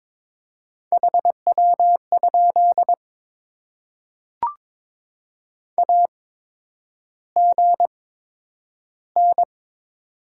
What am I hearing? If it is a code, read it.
Morse code, heard as a keyed tone, about 22 wpm, spaced as for 9 wpm, 700 Hz. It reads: HW? AGN
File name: Top 20 QSO Elements - 8x Character Spacing - 22wpm